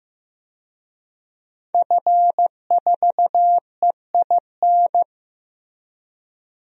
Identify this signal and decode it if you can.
Morse code, heard as a keyed tone, about 15 wpm, 700 Hz. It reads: F4EIN
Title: DXCC-100 Call Signs - 5 Characters - Repeated 2x - 15wpm